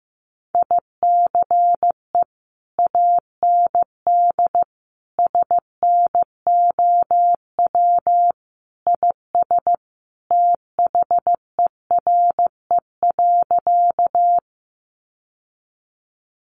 Morse code, heard as a keyed tone, about 15 wpm, 700 Hz. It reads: ICE AND SNOW IS THERE.
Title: Sentences from Top 1000 Words 15wpm